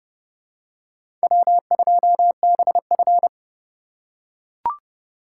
Morse code, heard as a keyed tone, about 28 wpm, 700 Hz. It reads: W2BF